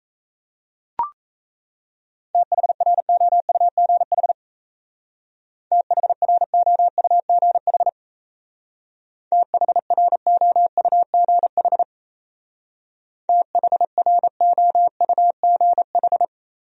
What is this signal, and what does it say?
Morse code, heard as a keyed tone, about 33 wpm, 700 Hz. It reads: THROUGH THROUGH THROUGH THROUGH
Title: Speed-Racing Top 500 Words 28wpm